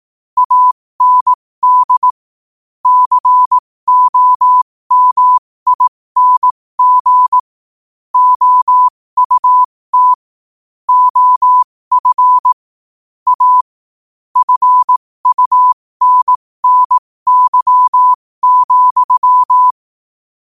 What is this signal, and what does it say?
Morse code, heard as a keyed tone, about 18 wpm, spaced as for 15 wpm, 1000 Hz. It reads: AND COMING OUT OF A FUNNY,